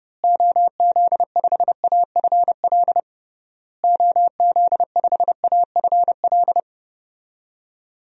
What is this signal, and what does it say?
Morse code, heard as a keyed tone, about 30 wpm, 700 Hz. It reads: OZ5AFL OZ5AFL
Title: DXCC-100 Call Signs - 6 Characters - Repeated 3x - 30wpm